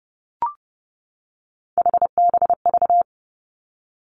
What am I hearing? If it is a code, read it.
Morse code, heard as a keyed tone, about 31 wpm, 700 Hz. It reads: HBV
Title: Three Challenging Letter-Number Combinations 30wpm